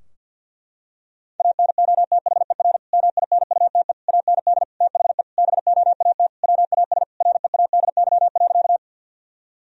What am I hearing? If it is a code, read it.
Morse code, heard as a keyed tone, about 50 wpm, 700 Hz. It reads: ANOTHER MINUTE AND THE BOAT WAS READY.
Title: Sentences from Top 1000 Words - Review Part 3 - 50wpm